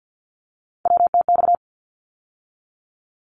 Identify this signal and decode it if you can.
Morse code, heard as a keyed tone, about 50 wpm, 700 Hz. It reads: WTX
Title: ARRL Sections 50wpm